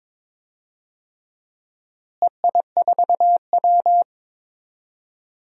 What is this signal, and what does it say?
Morse code, heard as a keyed tone, about 23 wpm, 700 Hz. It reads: EI4W
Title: DXCC-100 Call Signs - 4 Characters - Repeated 3x - 22wpm